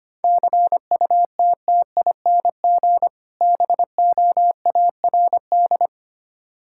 Morse code, heard as a keyed tone, about 25 wpm, 700 Hz. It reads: CUTTING BOARD
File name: practice / 25wpm / Difficult Words - Speed-Racing with Hints - 25wpm